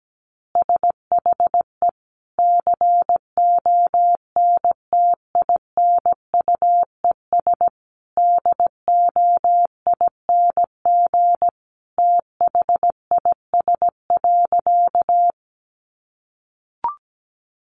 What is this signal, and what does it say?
Morse code, heard as a keyed tone, about 17 wpm, 700 Hz. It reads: SHE CONTINUES DOING THIS.
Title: Easy Phrases 17wpm